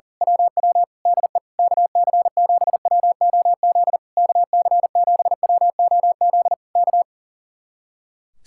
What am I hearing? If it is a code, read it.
Morse code, heard as a keyed tone, about 37 wpm, 700 Hz. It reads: WW DE KC7WOZ KC7WOZ K
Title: Speed-Racing Common QSO Phrases Encore - 40wpm